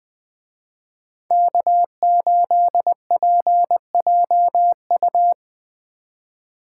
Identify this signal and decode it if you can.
Morse code, heard as a keyed tone, about 24 wpm, 700 Hz. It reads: K8PJU
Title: Speed Racing - USA 1x3 Call Signs 20wpm